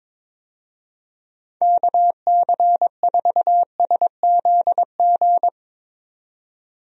Morse code, heard as a keyed tone, about 22 wpm, 700 Hz. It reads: KC4SZG
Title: USA Call Signs - 6 Characters - 22wpm